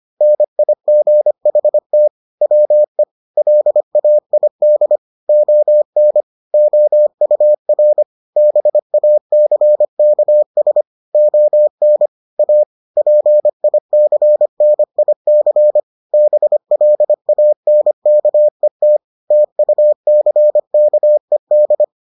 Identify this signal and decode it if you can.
Morse code, heard as a keyed tone, about 25 wpm, 600 Hz. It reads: NIGHT WE LAID ON OUR BACKS ON A PICNIC BLANKET TUCKED